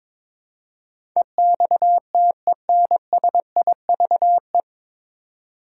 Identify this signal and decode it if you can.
Morse code, heard as a keyed tone, about 22 wpm, 700 Hz. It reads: EXTENSIVE
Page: Morse Code Ninja practice 22wpm